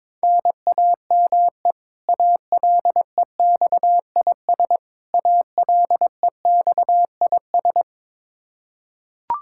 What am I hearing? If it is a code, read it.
Morse code, heard as a keyed tone, about 22 wpm, 700 Hz. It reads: NAME ALEXIS ALEXIS
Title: Practice QSO - Exchange 1 - Beginning - 22wpm